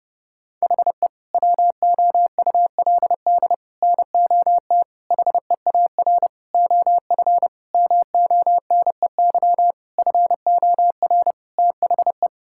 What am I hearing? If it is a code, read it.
Morse code, heard as a keyed tone, about 31 wpm, 700 Hz. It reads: HE WOULD NOT HEAR OF MONEY FOR THE
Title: Sentences from Top 1000 Words - Review Part 5 - 30wpm